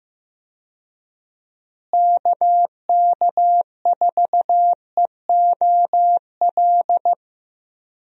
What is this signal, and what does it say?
Morse code, heard as a keyed tone, about 15 wpm, 700 Hz. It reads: KK4EOL
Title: USA Call Signs - 6 Characters - Repeated 3x - 15wpm